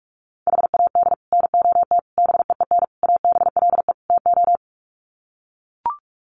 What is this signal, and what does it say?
Morse code, heard as a keyed tone, about 44 wpm, 700 Hz. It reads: HAD NOT BEEN ABLE TO
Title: Sets of 5 Words Encore 45wpm